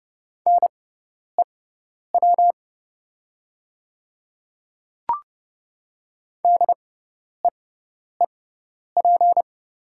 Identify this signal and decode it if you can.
Morse code, heard as a keyed tone, about 29 wpm, spaced as for 10 wpm, 700 Hz. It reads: NEW DEEP
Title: 6x Character Spacing - Top 500 Words - 30wpm